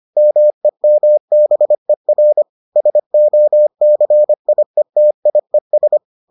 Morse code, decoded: MEMBER SOCIETIES

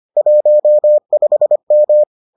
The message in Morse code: 15M

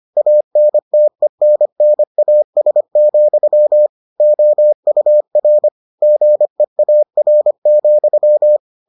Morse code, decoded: ANTENNAS, OUR GEAR,

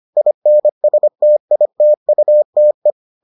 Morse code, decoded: INSTITUTE

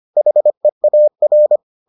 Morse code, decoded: HEAR

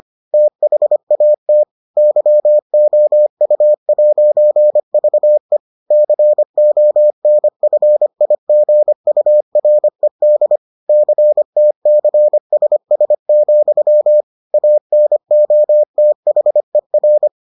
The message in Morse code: THAT YOU'VE CONFIGURED CTCSS, ANOTHER

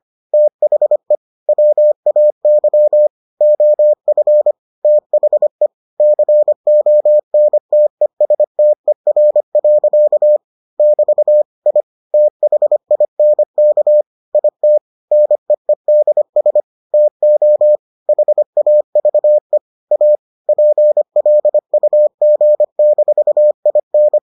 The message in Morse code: THE WAY OF THE CONTESTER. <BT> I THINK IT NEEDS TO HAVE A PLUG-IN

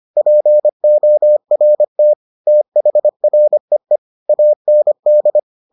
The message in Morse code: PORT THREE AND

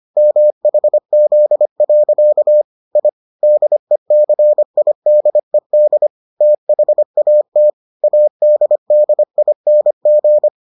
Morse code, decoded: MHZ. I DECIDED THAT ADDING

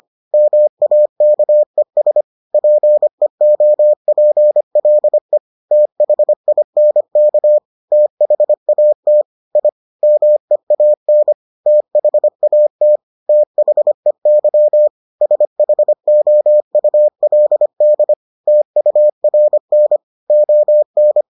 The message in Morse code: MAKES PEOPLE THINK THAT I MEAN THAT THEY SHOULD TURN ON

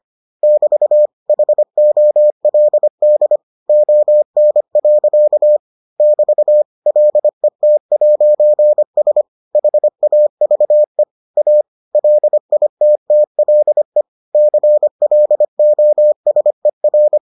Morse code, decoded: <BT> HOLD ON. <BT> LET'S HAVE A LITTLE CLOSER

